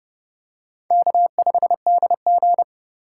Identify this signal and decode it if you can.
Morse code, heard as a keyed tone, about 30 wpm, 700 Hz. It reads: K5DG
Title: USA Call Signs - 4 Characters - 4x Character Spacing - 30wpm